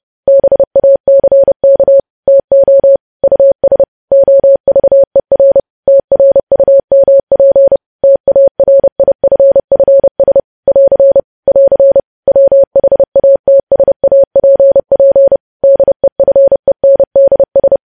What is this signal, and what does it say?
Morse code, heard as a keyed tone, about 30 wpm, 550 Hz. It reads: BACK TO US OVER TRUMP TARIFFS <AR> <AR> WHATSAPP DEFENDS
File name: podcast / News headlines-1x-30x30